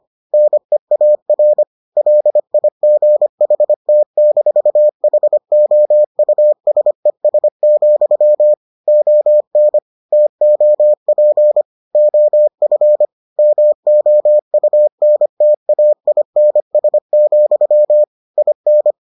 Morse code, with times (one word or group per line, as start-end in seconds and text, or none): 0.3-1.6 NEAR
2.0-8.5 LIGHT-HOUSES,
8.9-9.8 ON
10.1-11.6 TOP
11.9-13.1 OF
13.4-18.0 MOUNTAINS,
18.4-18.9 IN